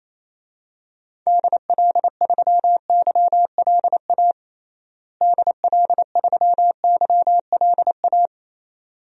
Morse code, heard as a keyed tone, about 28 wpm, 700 Hz. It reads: DL3YLA DL3YLA
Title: DXCC-100 Call Signs - 6 Characters - Repeated 2x - 28wpm